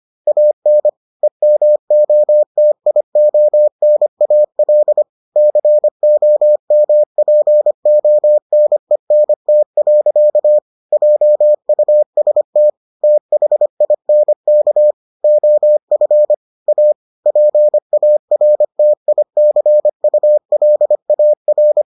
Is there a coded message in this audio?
Yes: Morse code from 0.3 to 21.8 s, heard as a keyed tone, about 25 wpm, 600 Hz. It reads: AN EMOTIONAL COMPONENT. JUST THINK OF A PARTICULAR